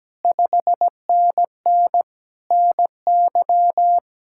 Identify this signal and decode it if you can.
Morse code, heard as a keyed tone, about 17 wpm, 700 Hz. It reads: 5NN NY
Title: Speed-Racing Common QSO Phrases - 17wpm